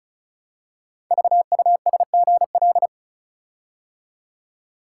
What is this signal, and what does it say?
Morse code, heard as a keyed tone, about 36 wpm, 700 Hz. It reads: VUSGL
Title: Code Groups - 5 Letters - 35wpm